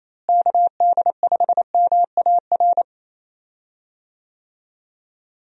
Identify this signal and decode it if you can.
Morse code, heard as a keyed tone, about 28 wpm, 700 Hz. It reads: KD5MAR
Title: USA Call Signs - 6 Characters - Repeated 3x - 28wpm